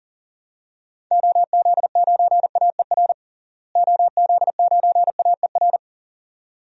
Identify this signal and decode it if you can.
Morse code, heard as a keyed tone, about 40 wpm, 700 Hz. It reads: OZ9AER OZ9AER
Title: DXCC-100 Call Signs - 6 Characters - Repeated 2x - 40wpm